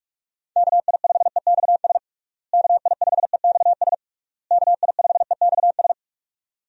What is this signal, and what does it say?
Morse code, heard as a keyed tone, about 46 wpm, 700 Hz. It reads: KI5EXS KI5EXS KI5EXS
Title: USA Call Signs - 6 Characters - Repeated 3x - 45wpm